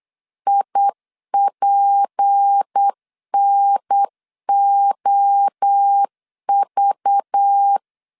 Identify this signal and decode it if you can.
Morse code, heard as a keyed tone, about 8 wpm, 800 Hz. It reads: IPNOV
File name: CW / DET / 1 6 1500C S B 8wpm 5142